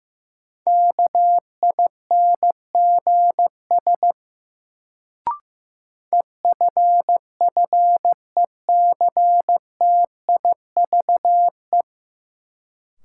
Morse code, heard as a keyed tone, about 15 wpm, 700 Hz. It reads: KINGS EFFECTIVE
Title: Sentences from Top 1000 Words - Review Part 6 - New Word Drill - 15wpm